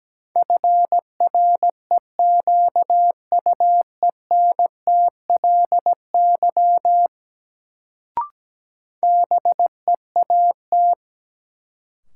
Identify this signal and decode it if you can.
Morse code, heard as a keyed tone, about 17 wpm, 700 Hz. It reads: FREQUENTLY BEAT
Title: Sentences from Top 1300 Words - New Word Drill Alt - 17wpm